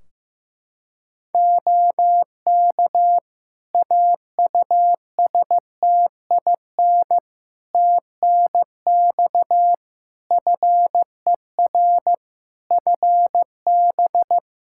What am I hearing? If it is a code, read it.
Morse code, heard as a keyed tone, about 15 wpm, 700 Hz. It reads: OK AUSTIN TNX FER FB
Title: Practice QSO - Exchange 4 - Ending - 15wpm